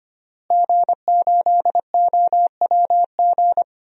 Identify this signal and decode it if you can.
Morse code, heard as a keyed tone, about 25 wpm, 700 Hz. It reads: G8OWG
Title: DXCC-100 Call Signs - 5 Characters - Repeated 3x - 25wpm